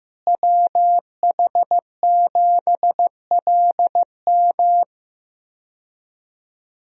Morse code, heard as a keyed tone, about 15 wpm, 700 Hz. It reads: WH7LM
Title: USA Call Signs - 5 Characters - Repeated 3x - 15wpm